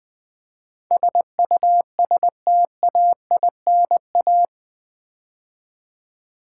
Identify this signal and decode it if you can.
Morse code, heard as a keyed tone, about 20 wpm, 700 Hz. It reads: SUSTAINA
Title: Build A Word - 4 Letters At A Time - 11-Letter Words - 20wpm